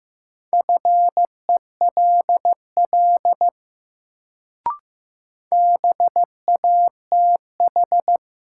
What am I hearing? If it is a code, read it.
Morse code, heard as a keyed tone, about 15 wpm, 700 Hz. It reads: FELL BATH
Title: Sentences from Top 1600 Words - New Word Drill - 15wpm